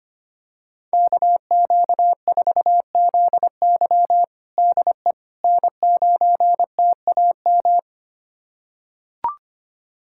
Morse code, heard as a keyed tone, about 25 wpm, 700 Hz. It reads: KQ4ZY DE N9TAM
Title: USA Callsign DE Callsign - 5 Characters - 25wpm